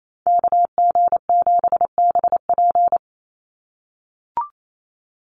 Morse code, heard as a keyed tone, about 28 wpm, 700 Hz. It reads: KG7BP
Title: USA Call Signs 28wpm